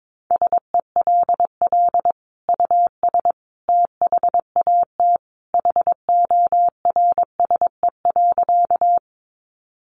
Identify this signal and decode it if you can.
Morse code, heard as a keyed tone, about 22 wpm, 700 Hz. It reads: SELL US THAT HORSE.
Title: Sentences from Top 1100 Words 22wpm